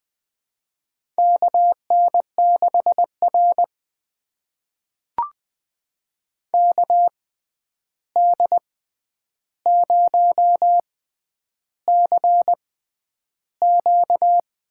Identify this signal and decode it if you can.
Morse code, heard as a keyed tone, about 20 wpm, spaced as for 10 wpm, 700 Hz. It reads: KN6R KD0CQ
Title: USA Call Signs - 6x Character Spacing - 20wpm